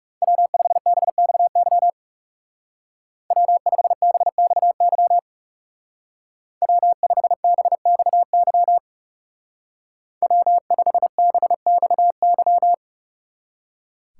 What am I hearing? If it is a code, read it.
Morse code, heard as a keyed tone, about 35 wpm, 700 Hz. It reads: W5BXY W5BXY W5BXY W5BXY
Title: Speed Racing - USA 1x3 Call Signs 30wpm